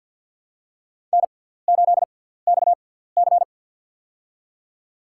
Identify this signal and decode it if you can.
Morse code, heard as a keyed tone, about 47 wpm, spaced as for 17 wpm, 700 Hz. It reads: N8XC
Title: USA Call Signs - 6x Character Spacing - 50wpm